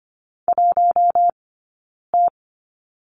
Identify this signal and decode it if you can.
Morse code, heard as a keyed tone, about 25 wpm, 700 Hz. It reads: 1 T
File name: course / 25:10wpm / 092 Lesson 16d - Letter c - Two Characters 25wpm